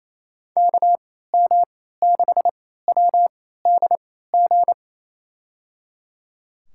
Morse code, heard as a keyed tone, about 28 wpm, spaced as for 16 wpm, 700 Hz. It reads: KM6WDG
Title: USA Call Signs - 6 Characters - 3x Character Spacing - 28wpm